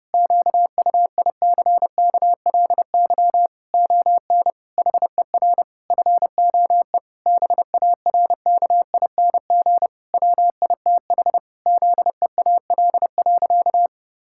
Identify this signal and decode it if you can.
Morse code, heard as a keyed tone, about 30 wpm, 700 Hz. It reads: QUICKLY ON HER FOE BARKING WITH ZEAL.
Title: Pangram Challenge v2 - 30wpm